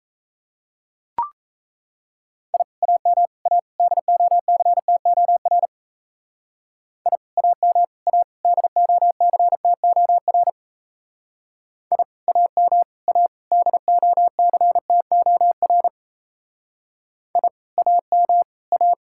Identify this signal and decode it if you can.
Morse code, heard as a keyed tone, about 33 wpm, 700 Hz. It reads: I AM A DOCTOR I AM A DOCTOR I AM A DOCTOR I AM A
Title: Speed-Racing Common QSO Phrases - 28wpm